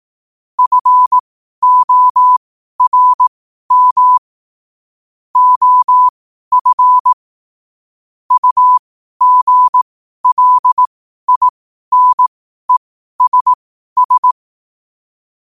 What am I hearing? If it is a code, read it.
Morse code, heard as a keyed tone, about 18 wpm, spaced as for 12 wpm, 1000 Hz. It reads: FORM OF UGLINESS
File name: qotdfiles / 12wpm